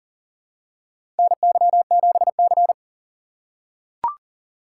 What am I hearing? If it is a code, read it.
Morse code, heard as a keyed tone, about 40 wpm, 700 Hz. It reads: NY7C
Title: USA Call Signs - 2x Character Spacing - 40wpm